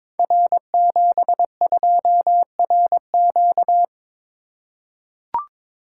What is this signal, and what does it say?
Morse code, heard as a keyed tone, about 22 wpm, 700 Hz. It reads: R72RQ
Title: Code Groups - 5 Letter:Numbers - 22wpm